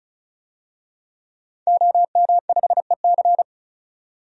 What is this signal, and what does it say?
Morse code, heard as a keyed tone, about 34 wpm, 700 Hz. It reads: OM5EC
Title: DXCC-100 Call Signs - 5 Characters - 35wpm